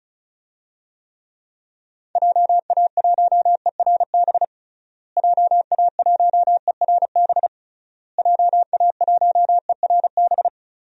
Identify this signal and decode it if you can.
Morse code, heard as a keyed tone, about 36 wpm, 700 Hz. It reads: JA1ERB JA1ERB JA1ERB
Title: DXCC-100 Call Signs - 6 Characters - Repeated 3x - 35wpm